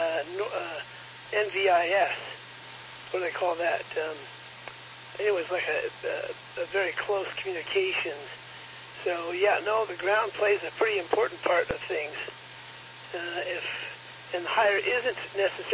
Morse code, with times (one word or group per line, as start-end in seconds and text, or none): none